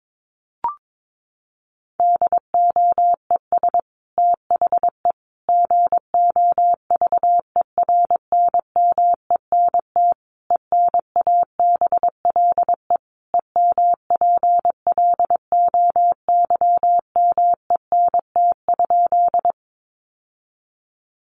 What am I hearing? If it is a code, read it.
Morse code, heard as a keyed tone, about 22 wpm, 700 Hz. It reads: DOES THE GOVERNMENT ENABLE EMPLOYMENT?
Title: Sentences from Top 1500 Words 22wpm